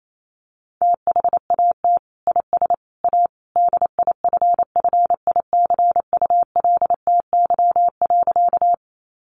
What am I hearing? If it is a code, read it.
Morse code, heard as a keyed tone, about 28 wpm, 700 Hz. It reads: THAT IS A DIFFICULTY.
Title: Sentences from Top 1500 Words 28wpm